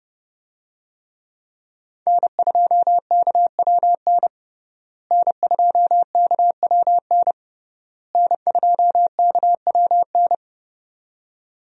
Morse code, heard as a keyed tone, about 30 wpm, 700 Hz. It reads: N2KWN N2KWN N2KWN
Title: USA Call Signs - 5 Characters - Repeated 3x - 30wpm